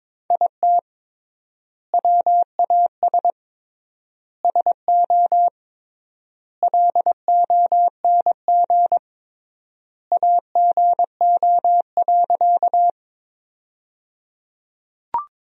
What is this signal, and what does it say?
Morse code, heard as a keyed tone, about 22 wpm, 700 Hz. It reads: IT WAS SO LONG AGO.
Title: 3x Word-Spacing - Sentences from Top 500 Words - 22wpm